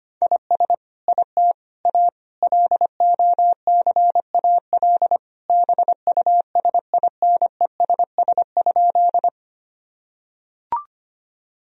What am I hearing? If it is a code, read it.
Morse code, heard as a keyed tone, about 25 wpm, 700 Hz. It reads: IS IT A LOCAL BUSINESS?
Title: Sentences from Top 1000 Words - Review Part 3 - 25wpm